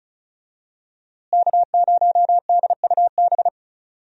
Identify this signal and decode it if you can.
Morse code, heard as a keyed tone, about 36 wpm, 700 Hz. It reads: K0DUB
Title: USA Call Signs - 5 Characters - Repeated 2x - 35wpm